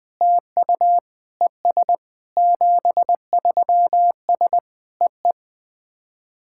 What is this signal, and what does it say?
Morse code, heard as a keyed tone, about 20 wpm, 700 Hz. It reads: TU ES 73S EE
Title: SOTA QSO Practice 20wpm